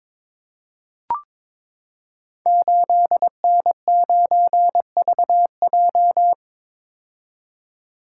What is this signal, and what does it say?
Morse code, heard as a keyed tone, about 22 wpm, 700 Hz. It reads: 8N9VJ
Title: Code Groups - 5 Letter:Numbers - 22wpm